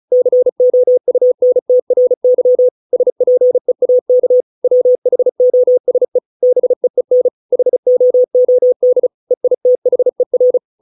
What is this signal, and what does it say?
Morse code, heard as a keyed tone, about 34 wpm, 500 Hz. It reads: COUNTRY SPEAK WHOSE BEEN HOOD EITHER